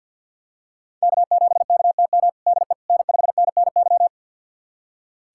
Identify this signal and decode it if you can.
Morse code, heard as a keyed tone, about 47 wpm, 700 Hz. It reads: K7KTM DE N5NNY
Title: USA Callsign DE Callsign 50wpm